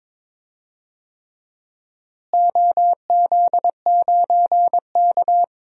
Morse code, heard as a keyed tone, about 22 wpm, 700 Hz. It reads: OZ9K
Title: DXCC-100 Call Signs - 4 Characters - Repeated 3x - 22wpm